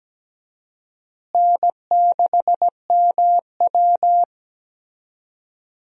Morse code, heard as a keyed tone, about 17 wpm, 700 Hz. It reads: N6MW
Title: USA Call Signs - 4 Characters - Repeated 3x - 17wpm